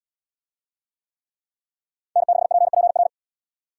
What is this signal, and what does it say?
Morse code, heard as a keyed tone, about 70 wpm, 700 Hz. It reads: N6PPR